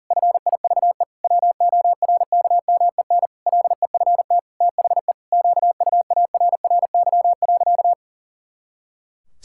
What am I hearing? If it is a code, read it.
Morse code, heard as a keyed tone, about 37 wpm, 700 Hz. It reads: FIVE WORKMEN LEFT THE QUARRY.
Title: Pangram Challenge v2 - 40wpm